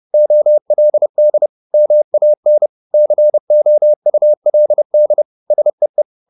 Morse code, heard as a keyed tone, about 30 wpm, 600 Hz. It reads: OLD MAN COULD SEE